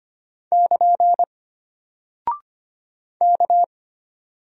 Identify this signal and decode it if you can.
Morse code, heard as a keyed tone, about 25 wpm, 700 Hz. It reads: <KN> K
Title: Single Character (Letter:Number:Punctuation:Prosign) Alt 25wpm